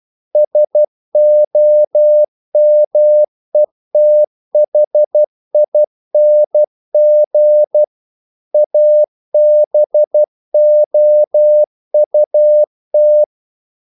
Morse code, heard as a keyed tone, about 12 wpm, 600 Hz. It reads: SOMETHING ABOUT
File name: CW / Practice / EnglishQuotations0112